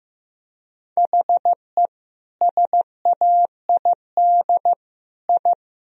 Morse code, heard as a keyed tone, about 16 wpm, 700 Hz. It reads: HE SAID I
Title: Sentences from Top 1000 Words - Review Part 4 - 15wpm